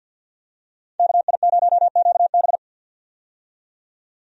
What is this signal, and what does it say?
Morse code, heard as a keyed tone, about 51 wpm, 700 Hz. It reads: KI0QB
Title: USA Call Signs - 5 Characters - 50wpm